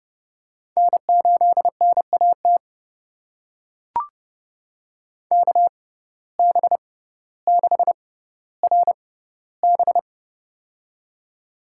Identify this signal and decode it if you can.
Morse code, heard as a keyed tone, about 30 wpm, spaced as for 15 wpm, 700 Hz. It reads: N8NAT KB6RB
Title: USA Call Signs - 5 Characters - 6x Character Spacing - 30wpm